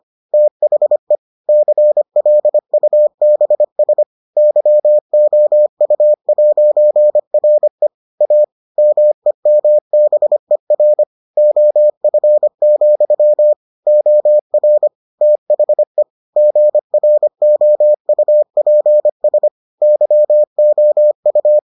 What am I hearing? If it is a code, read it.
Morse code, heard as a keyed tone, about 25 wpm, 600 Hz. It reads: THE CLUBS YOU'RE A MEMBER OF, OR THE GROUPS YOU